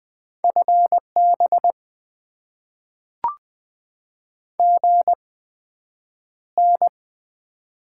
Morse code, heard as a keyed tone, about 21 wpm, spaced as for 9 wpm, 700 Hz. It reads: FB GN
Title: Top 90 QSO Elements - 8x Character Spacing - 20wpm